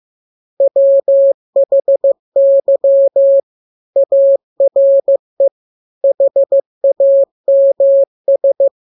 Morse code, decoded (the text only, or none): WHY ARE HAMS